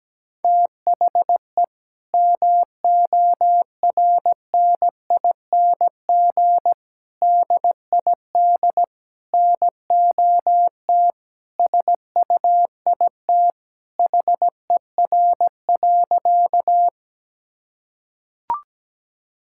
Morse code, heard as a keyed tone, about 17 wpm, 700 Hz. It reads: THE MORNING DID NOT SUIT HER.
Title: Sentences from Top 1600 Words 17wpm